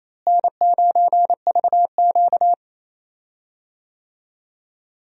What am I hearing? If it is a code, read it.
Morse code, heard as a keyed tone, about 28 wpm, 700 Hz. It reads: N9VQ
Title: USA Call Signs - 4 Characters - Repeated 2x - 28wpm